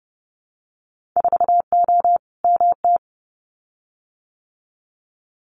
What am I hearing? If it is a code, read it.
Morse code, heard as a keyed tone, about 30 wpm, 700 Hz. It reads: 4O MT
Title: Winter Field Day Exchanges Encore 30wpm